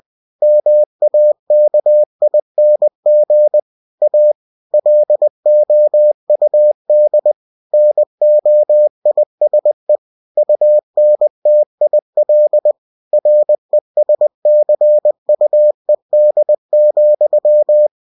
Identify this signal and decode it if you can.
Morse code, heard as a keyed tone, about 20 wpm, 600 Hz. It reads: MAKING A LOUD NOISE UNTIL RESCUED,
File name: cw-books / winnie-the-pooh / winnie-20wpm-page-0091